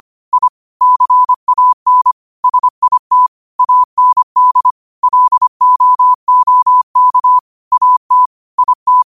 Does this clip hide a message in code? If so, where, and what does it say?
Yes: Morse code from 0.3 to 9.0 s, heard as a keyed tone, about 25 wpm, 1000 Hz. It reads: I CAN SIT AND LOOK AT IT